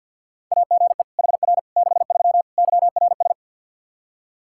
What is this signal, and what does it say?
Morse code, heard as a keyed tone, about 52 wpm, 700 Hz. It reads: AGE HR 63 YRS
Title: Practice QSO - Exchange 3 - More Info - 50wpm